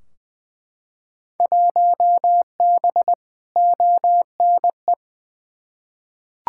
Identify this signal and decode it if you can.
Morse code, heard as a keyed tone, about 20 wpm, 700 Hz. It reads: 1B ONE
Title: Field Day Exchanges - 20wpm